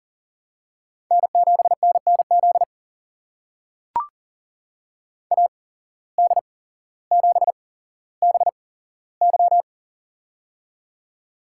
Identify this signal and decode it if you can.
Morse code, heard as a keyed tone, about 40 wpm, spaced as for 17 wpm, 700 Hz. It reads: N7NNZ AD7BY
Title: USA Call Signs - 5 Characters - 8x Character Spacing - 40wpm